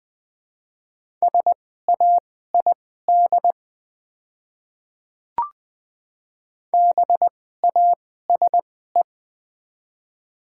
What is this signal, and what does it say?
Morse code, heard as a keyed tone, about 20 wpm, spaced as for 14 wpm, 700 Hz. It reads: SAID BASE
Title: 2x Character Spacing - Top 300 Words - 20wpm